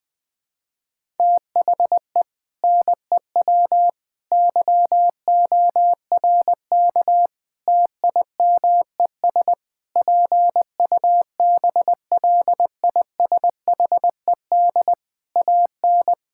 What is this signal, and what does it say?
Morse code, heard as a keyed tone, about 20 wpm, 700 Hz. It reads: THE NEW YORK TIMES PUBLISHED AN
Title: Sentences with 4-Word Phrases - 20wpm